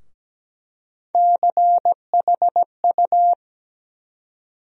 Morse code, heard as a keyed tone, about 17 wpm, 700 Hz. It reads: CHU
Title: USA Call Sign Postfixes - 3 Letters 17wpm